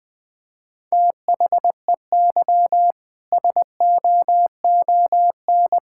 Morse code, heard as a keyed tone, about 20 wpm, 700 Hz. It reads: THEY SOON
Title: Sentences from Top 1000 Words - Review Part 5 - 20wpm